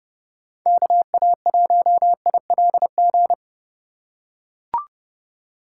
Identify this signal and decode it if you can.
Morse code, heard as a keyed tone, about 29 wpm, 700 Hz. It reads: KA1ILG